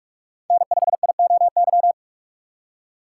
Morse code, heard as a keyed tone, about 44 wpm, 690 Hz. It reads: N5IOY
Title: USA Call Signs - 5 Characters - 2x Character Spacing - 45wpm